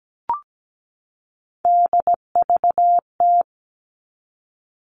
Morse code, heard as a keyed tone, about 17 wpm, 700 Hz. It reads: DVT
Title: All Cut Numbers - 3-Digit 17wpm